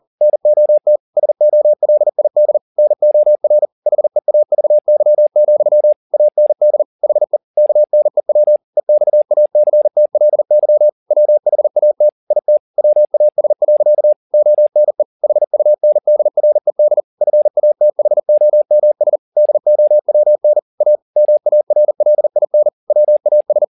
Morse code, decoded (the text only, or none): NOT SOLID NOR HEAVY, AND HE KNEW EXACTLY WHAT IT WAS. ONE HUNDRED FATHOMS DOWN A MARLIN WAS